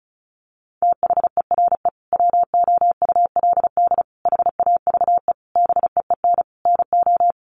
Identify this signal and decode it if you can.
Morse code, heard as a keyed tone, about 34 wpm, 700 Hz. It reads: THERE WOULD HAVE BEEN NO
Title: Sentences from Top 1400 Words 35wpm